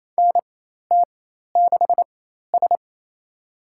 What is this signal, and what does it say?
Morse code, heard as a keyed tone, about 28 wpm, spaced as for 13 wpm, 700 Hz. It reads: NT6S
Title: USA Call Signs - 4 Characters - 4x Character Spacing - 28wpm